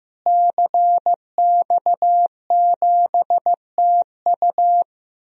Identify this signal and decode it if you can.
Morse code, heard as a keyed tone, about 15 wpm, 700 Hz. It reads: CX7TU
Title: DXCC-100 Call Signs - 5 Characters - Repeated 3x - 15wpm